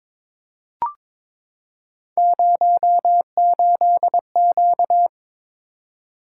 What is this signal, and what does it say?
Morse code, heard as a keyed tone, about 22 wpm, 700 Hz. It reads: 08Q
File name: practice / 22wpm / Three Letter-Number Combinations - 22wpm